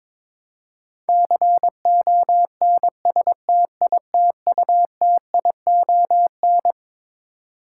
Morse code, heard as a keyed tone, about 22 wpm, 700 Hz. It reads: CONSTITUTION
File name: Twelve Letter Words - 22wpm